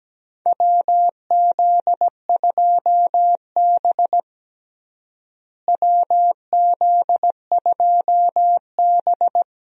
Morse code, heard as a keyed tone, about 17 wpm, 700 Hz. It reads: WZ2B WZ2B
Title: USA Call Signs - 4 Characters - Repeated 2x - 17wpm